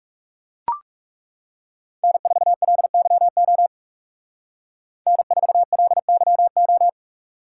Morse code, heard as a keyed tone, about 37 wpm, 700 Hz. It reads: N4LYO N4LYO
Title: Speed Racing - USA 1x3 Call Signs 30wpm